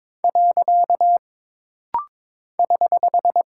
Morse code, heard as a keyed tone, about 22 wpm, 700 Hz. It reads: . <HH>
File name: practice / 22wpm / Single Punctuation-Prosign - Rapid-Fire - 22wpm